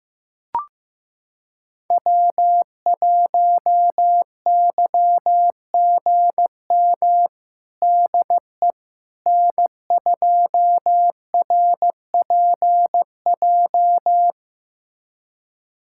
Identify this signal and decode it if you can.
Morse code, heard as a keyed tone, about 15 wpm, 700 Hz. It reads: W1YGM DE N2RPJ
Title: USA Callsign DE Callsign - 5 Characters - 15wpm